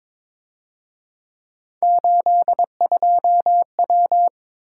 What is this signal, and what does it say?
Morse code, heard as a keyed tone, about 26 wpm, 700 Hz. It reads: 82W
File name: Speed-Racing - Three Letter-Number Combinations - 22wpm